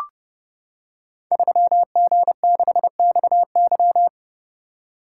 Morse code, heard as a keyed tone, about 30 wpm, 700 Hz. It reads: 3G6XY
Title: Code Groups - 5 Letter:Numbers - 30wpm